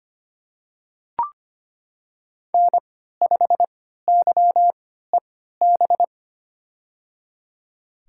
Morse code, heard as a keyed tone, about 25 wpm, spaced as for 14 wpm, 700 Hz. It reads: N5YEB
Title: USA Call Signs - 3x Character Spacing - 25wpm